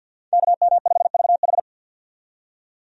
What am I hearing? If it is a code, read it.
Morse code, heard as a keyed tone, about 50 wpm, 700 Hz. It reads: KM5VH
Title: USA Call Signs - 5 Characters - 50wpm